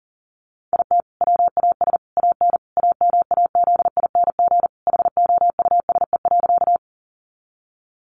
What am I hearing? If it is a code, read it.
Morse code, heard as a keyed tone, about 40 wpm, 700 Hz. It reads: IT WAS AN AMAZING HOUSE.
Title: Sentences from Top 1500 Words 40wpm